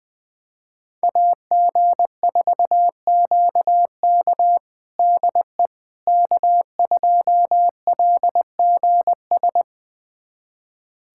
Morse code, heard as a keyed tone, about 21 wpm, 700 Hz. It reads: AG4QK DE K2LGS
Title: USA Callsign DE Callsign - 5 Characters - 20wpm